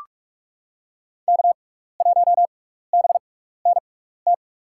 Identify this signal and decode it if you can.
Morse code, heard as a keyed tone, about 46 wpm, spaced as for 15 wpm, 690 Hz. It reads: K1BNT
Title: USA Call Signs - 5 Characters - 6x Character Spacing - 45wpm